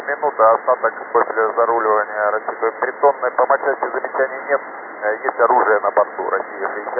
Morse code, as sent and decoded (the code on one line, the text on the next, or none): none